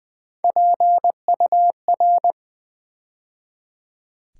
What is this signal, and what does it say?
Morse code, heard as a keyed tone, about 20 wpm, 700 Hz. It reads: PUR
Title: Build A Word - 3 Letters At A Time - 8-Letter Words - 20wpm